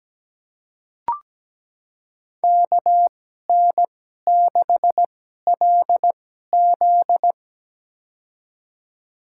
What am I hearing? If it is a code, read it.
Morse code, heard as a keyed tone, about 17 wpm, spaced as for 12 wpm, 700 Hz. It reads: KN6LZ
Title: USA Call Signs - 5 Characters - 2x Character Spacing - 17wpm